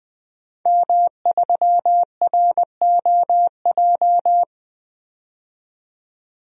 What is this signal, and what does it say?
Morse code, heard as a keyed tone, about 20 wpm, 700 Hz. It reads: M3ROJ
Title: DXCC-100 Call Signs - 5 Characters - 20wpm